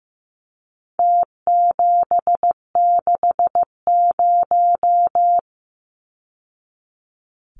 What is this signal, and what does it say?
Morse code, heard as a keyed tone, about 15 wpm, 700 Hz. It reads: T760
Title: Numbers - Contest Style - 4-Digit 15wpm